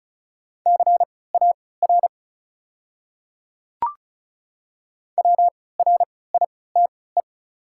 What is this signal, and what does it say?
Morse code, heard as a keyed tone, about 36 wpm, spaced as for 20 wpm, 700 Hz. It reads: CAR WRITE
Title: 3x Character Spacing - Top 500 Words - 35wpm